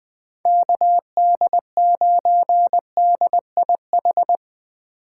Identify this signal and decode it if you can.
Morse code, heard as a keyed tone, about 19 wpm, 700 Hz. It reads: KD9DIH
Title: Speed Racing - USA 2x3 Call Signs 17wpm